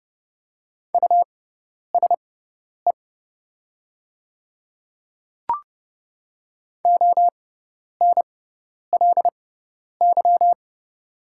Morse code, heard as a keyed tone, about 30 wpm, spaced as for 10 wpm, 690 Hz. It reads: USE ONLY
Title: 6x Character Spacing - Top 200 Words - 30wpm